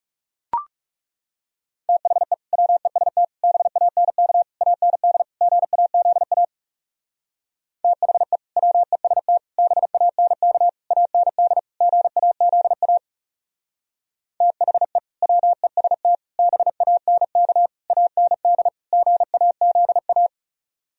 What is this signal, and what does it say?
Morse code, heard as a keyed tone, about 36 wpm, 700 Hz. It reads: THE WEST BANK AND GAZA THE WEST BANK AND GAZA THE WEST BANK AND GAZA
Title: Speed-Racing - Sets of 5 Words 30wpm